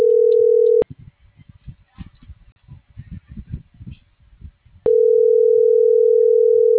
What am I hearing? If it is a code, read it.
Ringback tone (North America), 440 + 480 Hz.
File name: CENTRAL FRALSE / out gerardo2024sub